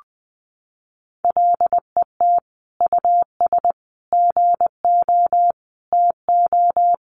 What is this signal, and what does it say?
Morse code, heard as a keyed tone, about 20 wpm, 700 Hz. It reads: LET US GO TO
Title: Sentences from Top 1500 Words 20wpm